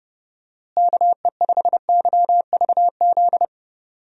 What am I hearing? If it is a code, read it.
Morse code, heard as a keyed tone, about 30 wpm, 700 Hz. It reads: KE5YVZ
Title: USA Call Signs - 6 Characters - Repeated 3x - 30wpm